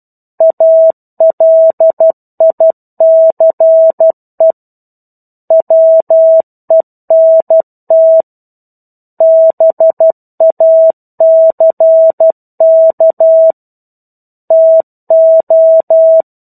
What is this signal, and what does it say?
Morse code, heard as a keyed tone, about 12 wpm, 650 Hz. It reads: ALICE WENT BACK TO
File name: 01-Chapter-01 12 WPM